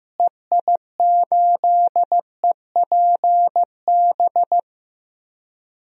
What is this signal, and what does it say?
Morse code, heard as a keyed tone, about 15 wpm, 700 Hz. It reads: EI8EPB
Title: DXCC-100 Call Signs - 6 Characters - Repeated 2x - 15wpm